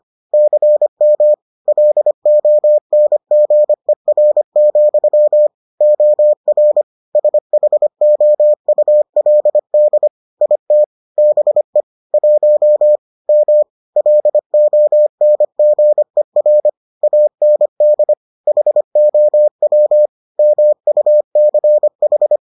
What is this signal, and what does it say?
Morse code, heard as a keyed tone, about 25 wpm, 600 Hz. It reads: CM LONGER, OR SHOULD IT BE 1 M LONGER AND HOW MUCH